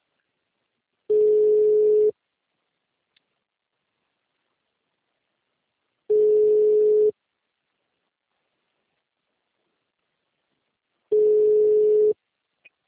Ringback tone (Europe, 425 Hz).
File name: out- -2009- - - .466